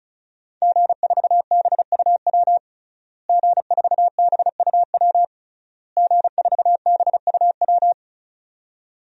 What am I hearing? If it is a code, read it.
Morse code, heard as a keyed tone, about 36 wpm, 700 Hz. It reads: G4BUW G4BUW G4BUW
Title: DXCC-100 Call Signs - 5 Characters - Repeated 3x - 35wpm